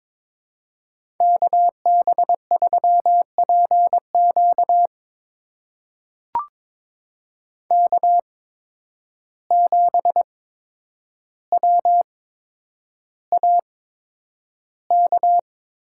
Morse code, heard as a keyed tone, about 22 wpm, spaced as for 9 wpm, 700 Hz. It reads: KB3PQ K7WAK
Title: USA Call Signs - 5 Characters - 8x Character Spacing - 22wpm